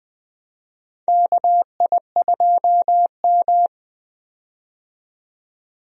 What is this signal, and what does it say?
Morse code, heard as a keyed tone, about 20 wpm, 700 Hz. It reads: KI2M
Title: USA Call Signs - 4 Characters - 20wpm